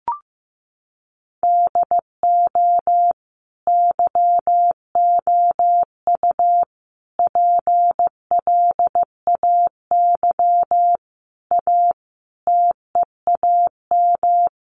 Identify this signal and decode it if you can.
Morse code, heard as a keyed tone, about 15 wpm, 700 Hz. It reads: DO YOU PLAY A TEAM
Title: Sentences from Top 1500 Words 15wpm